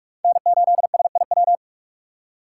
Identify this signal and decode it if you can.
Morse code, heard as a keyed tone, about 44 wpm, 700 Hz. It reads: N8SIW
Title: USA Call Signs - 6x Character Spacing - 45wpm